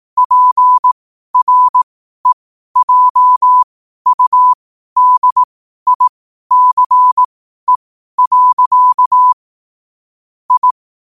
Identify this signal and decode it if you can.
Morse code, heard as a keyed tone, about 18 wpm, spaced as for 12 wpm, 1000 Hz. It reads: PREJUDICE. I
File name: qotdfiles / 12wpm